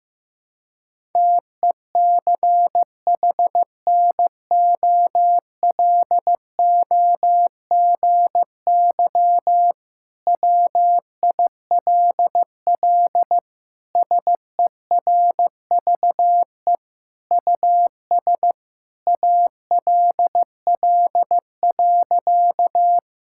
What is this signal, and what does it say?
Morse code, heard as a keyed tone, about 16 wpm, 690 Hz. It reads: TECHNOLOGY WILL SERVE US ALL.
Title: Sentences from Top 1000 Words - Review Part 5 - 15wpm